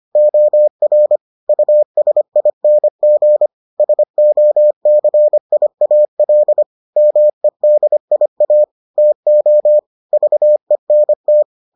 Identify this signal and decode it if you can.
Morse code, heard as a keyed tone, about 25 wpm, 600 Hz. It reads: OR USING SOCIAL MEDIA TO VENT